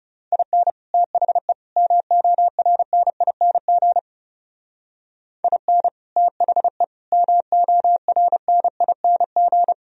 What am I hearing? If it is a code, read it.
Morse code, heard as a keyed tone, about 31 wpm, 700 Hz. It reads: IN THE MORNING IN THE MORNING
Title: Speed-Racing - Sets of 3 Words 30wpm